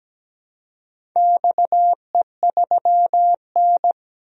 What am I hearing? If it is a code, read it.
Morse code, heard as a keyed tone, about 17 wpm, 700 Hz. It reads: XE3N
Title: DXCC-100 Call Signs - 4 Characters - 17wpm